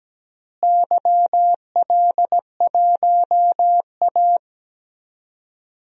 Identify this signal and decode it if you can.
Morse code, heard as a keyed tone, about 17 wpm, 700 Hz. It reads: YL1A
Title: DXCC-100 Call Signs - 4 Characters - Repeated 2x - 17wpm